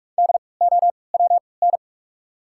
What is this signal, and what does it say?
Morse code, heard as a keyed tone, about 48 wpm, spaced as for 26 wpm, 700 Hz. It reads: DOWN
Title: 3x Character Spacing - Top 200 Words - 45wpm